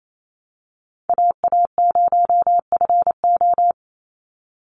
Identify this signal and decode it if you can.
Morse code, heard as a keyed tone, about 28 wpm, 700 Hz. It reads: AA0FO
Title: Call Signs Repeated 3x - 28wpm